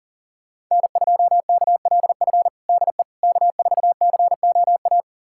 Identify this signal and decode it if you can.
Morse code, heard as a keyed tone, about 40 wpm, 700 Hz. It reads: N2KLF DE K4COA
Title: USA Callsign DE Callsign - 5 Characters - 40wpm